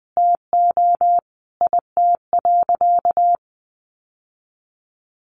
Morse code, heard as a keyed tone, about 20 wpm, 700 Hz. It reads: TO IT.
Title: Sentences from Top 1000 Words 20wpm